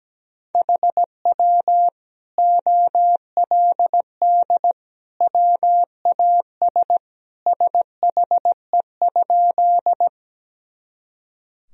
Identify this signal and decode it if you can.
Morse code, heard as a keyed tone, about 17 wpm, 700 Hz. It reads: HW OLD WAS SHE?
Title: Sentences from Top 200 Words and Abbreviations 17wpm